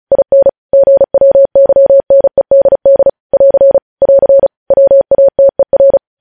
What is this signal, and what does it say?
Morse code, heard as a keyed tone, about 36 wpm, 550 Hz. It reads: IN GWYNEDD <AR> <AR> WATER